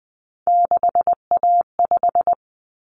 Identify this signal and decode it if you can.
Morse code, heard as a keyed tone, about 20 wpm, 700 Hz. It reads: 6A5
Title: Common Cut Numbers - 3-Digit 20wpm